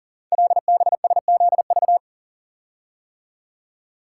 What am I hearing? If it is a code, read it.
Morse code, heard as a keyed tone, about 38 wpm, 690 Hz. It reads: LBSZV
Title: Code Groups - 5 Letters - 40wpm